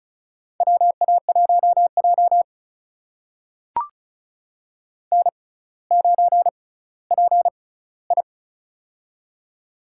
Morse code, heard as a keyed tone, about 36 wpm, spaced as for 18 wpm, 700 Hz. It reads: WA1J N9PI
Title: USA Call Signs - 4 Characters - 6x Character Spacing - 35wpm